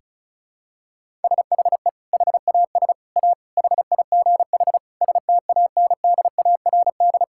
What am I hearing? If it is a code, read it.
Morse code, heard as a keyed tone, about 34 wpm, 700 Hz. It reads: SHE HAS A HIGH STANDARD
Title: Sentences from Top 1000 Words - Review Part 6 - 35wpm